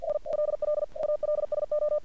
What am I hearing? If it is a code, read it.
Morse code, heard as a keyed tone, about 48 wpm, 600 Hz. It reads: ILL ALSO
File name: archived / CRSnet